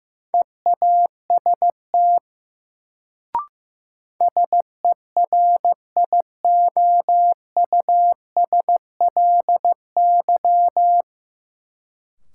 Morse code, heard as a keyed tone, about 15 wpm, 700 Hz. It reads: EAST SERIOUSLY
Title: Sentences from Top 1200 Words - New Word Drill Alt - 15wpm